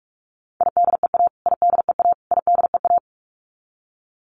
Morse code, heard as a keyed tone, about 44 wpm, 700 Hz. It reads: IDEA IDEA IDEA
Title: Top 500 Words Repeated 45wpm